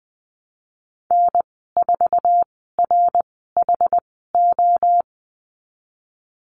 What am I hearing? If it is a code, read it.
Morse code, heard as a keyed tone, about 20 wpm, spaced as for 15 wpm, 700 Hz. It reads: N4RHO
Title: 076 Lesson 13g - Letter L - Call Signs 20wpm